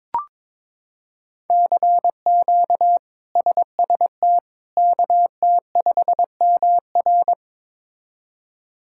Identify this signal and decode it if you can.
Morse code, heard as a keyed tone, about 22 wpm, 700 Hz. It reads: CQ SST KT5MR
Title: K1USN SST QSO Practice - 22wpm